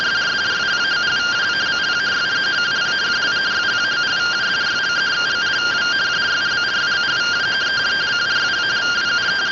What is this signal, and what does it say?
Two-tone teleprinter signal, about 170 Hz shift.
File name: findings / MR AXIAL 0mm 6mm.gen-37 rtty rx